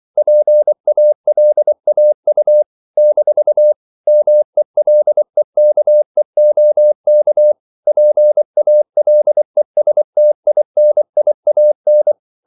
Morse code, heard as a keyed tone, about 24 wpm, 600 Hz. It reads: PALAU - MELEKEOK PALESTINIAN
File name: CW / Practice / NationalCapitals24